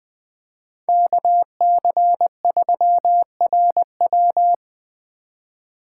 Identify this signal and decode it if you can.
Morse code, heard as a keyed tone, about 20 wpm, 700 Hz. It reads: KC3RW
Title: 2023 POTA Activator Call Signs - 20wpm